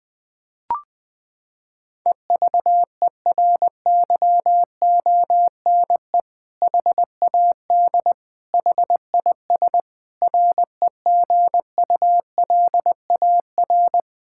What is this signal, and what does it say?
Morse code, heard as a keyed tone, about 20 wpm, 700 Hz. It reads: EVERYONE HAD HIS REGULAR
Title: Sentences from Top 1000 Words - Review Part 6 - 20wpm